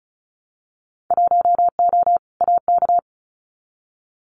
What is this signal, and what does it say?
Morse code, heard as a keyed tone, about 36 wpm, 700 Hz. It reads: 1O AK
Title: Winter Field Day Exchanges Encore 35wpm